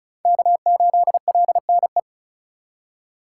Morse code, heard as a keyed tone, about 34 wpm, 700 Hz. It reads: K8LNE
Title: USA Call Signs - 5 Characters - 4x Character Spacing - 35wpm